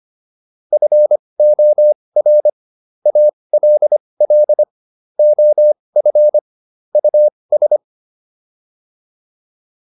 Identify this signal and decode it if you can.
Morse code, heard as a keyed tone, about 25 wpm, spaced as for 20 wpm, 600 Hz. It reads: FOR ALL OF US